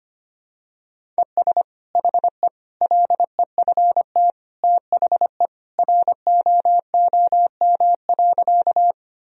Morse code, heard as a keyed tone, about 25 wpm, 700 Hz. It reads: ES HE LEFT THE ROOM.
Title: Sentences from Top 300 Words and Abbreviations 25wpm